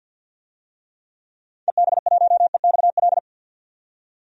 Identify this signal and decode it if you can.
Morse code, heard as a keyed tone, about 49 wpm, 700 Hz. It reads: EB1EXL
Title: DXCC-100 Call Signs - 6 Characters - 50wpm